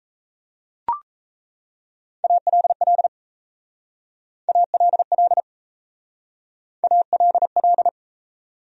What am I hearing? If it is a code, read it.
Morse code, heard as a keyed tone, about 35 wpm, 700 Hz. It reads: ALL ALL ALL
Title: Speed-Racing Top 500 Words 28wpm